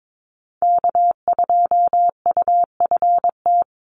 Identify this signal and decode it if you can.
Morse code, heard as a keyed tone, about 22 wpm, 700 Hz. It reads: K2UFT
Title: Call Signs Repeated 3x - 22wpm